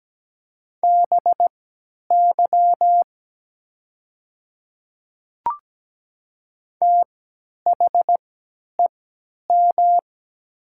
Morse code, heard as a keyed tone, about 17 wpm, spaced as for 10 wpm, 700 Hz. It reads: BY THEM
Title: 3x Character Spacing - Top 100 Words - 17wpm